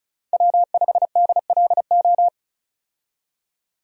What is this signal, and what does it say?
Morse code, heard as a keyed tone, about 34 wpm, 700 Hz. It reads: W5DLO
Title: USA Call Signs - 5 Characters - 35wpm